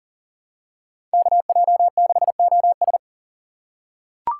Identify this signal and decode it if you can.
Morse code, heard as a keyed tone, about 40 wpm, 700 Hz. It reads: KJ6OS
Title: USA Call Signs - 5 Characters - 4x Character Spacing - 40wpm